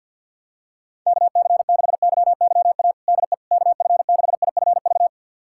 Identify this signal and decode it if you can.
Morse code, heard as a keyed tone, about 47 wpm, 700 Hz. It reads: KC6YYA DE KF6IFV
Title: USA Callsign DE Callsign - 6 Characters - 50wpm